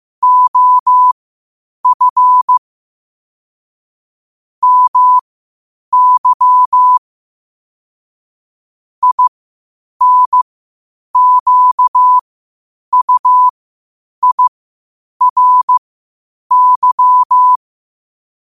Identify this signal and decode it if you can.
Morse code, heard as a keyed tone, about 15 wpm, spaced as for 8 wpm, 1000 Hz. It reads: OF MY INQUIRY
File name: qotdfiles / 8wpm